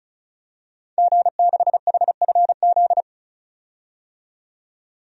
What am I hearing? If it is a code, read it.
Morse code, heard as a keyed tone, about 36 wpm, 700 Hz. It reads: G6HFZ
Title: DXCC-100 Call Signs - 5 Characters - 35wpm